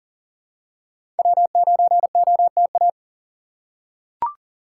Morse code, heard as a keyed tone, about 40 wpm, 700 Hz. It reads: W9OTA
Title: USA Call Signs - 2x Character Spacing - 40wpm